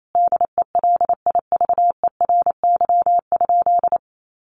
Morse code, heard as a keyed tone, about 28 wpm, 700 Hz. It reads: DELIVERY?
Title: Sentences from Top 1500 Words 28wpm